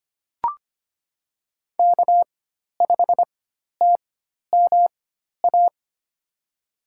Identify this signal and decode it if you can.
Morse code, heard as a keyed tone, about 25 wpm, spaced as for 12 wpm, 700 Hz. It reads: K5TMA
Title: USA Call Signs - 5 Characters - 4x Character Spacing - 25wpm